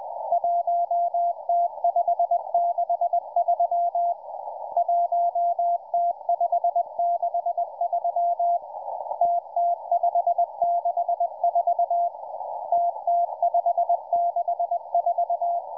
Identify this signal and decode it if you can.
Morse code, heard as a keyed tone, about 20 wpm, 690 Hz, with radio noise. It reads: 1T563 1T563 TT564 TT564